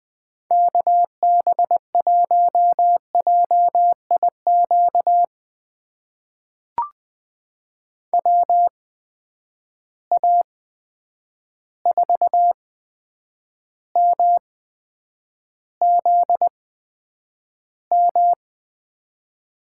Morse code, heard as a keyed tone, about 20 wpm, spaced as for 9 wpm, 700 Hz. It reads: KB1JIQ WA4MZM
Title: USA Call Signs - 6 Characters - 8x Character Spacing - 20wpm